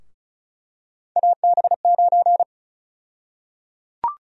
Morse code, heard as a keyed tone, about 36 wpm, 700 Hz. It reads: AB9
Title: USA Call Sign Prefixes - Repeated 2x - 35wpm